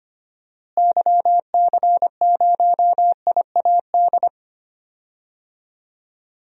Morse code, heard as a keyed tone, about 25 wpm, 700 Hz. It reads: YC0IAD
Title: DXCC-100 Call Signs - 6 Characters - Repeated 2x - 25wpm